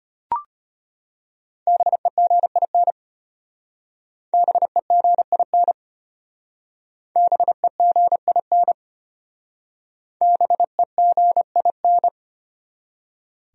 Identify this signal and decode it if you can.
Morse code, heard as a keyed tone, about 31 wpm, 700 Hz. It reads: BEGIN BEGIN BEGIN BEGIN